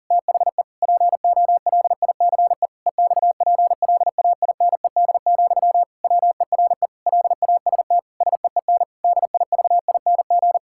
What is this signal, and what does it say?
Morse code, heard as a keyed tone, about 40 wpm, 700 Hz. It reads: THE POLICE EXPLAINED, WERE LAST SEEN DIVING
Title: Pangram Challenge v2 - 40wpm